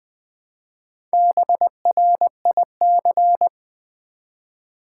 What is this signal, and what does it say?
Morse code, heard as a keyed tone, about 20 wpm, 700 Hz. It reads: BRIC
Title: Build A Word - 1 Letter At A Time - 5-Letter Words - 20wpm